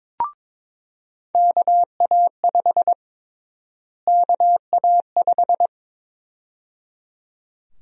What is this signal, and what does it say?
Morse code, heard as a keyed tone, about 22 wpm, 700 Hz. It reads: KA5 KA5
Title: USA Call Sign Prefixes - Repeated 2x - 22wpm